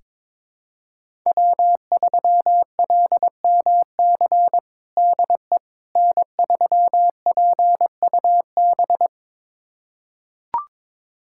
Morse code, heard as a keyed tone, about 21 wpm, 700 Hz. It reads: W3LMC DE N3PUB
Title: USA Callsign DE Callsign - 5 Characters - 22wpm